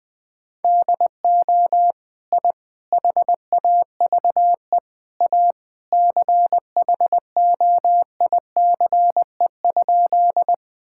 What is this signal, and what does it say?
Morse code, heard as a keyed tone, about 21 wpm, 700 Hz. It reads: DO I HAVE A CHOICE?
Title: Sentences from Top 1000 Words - Review Part 6 - 20wpm